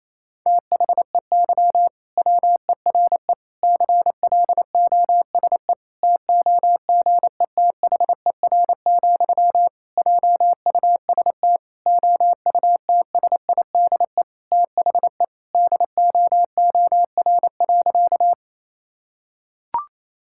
Morse code, heard as a keyed tone, about 27 wpm, 700 Hz. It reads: THEY WERE CLOSE TOGETHER, JUST OUTSIDE THE DOOR.
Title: Sentences from Top 1000 Words - Review Part 6 - 28wpm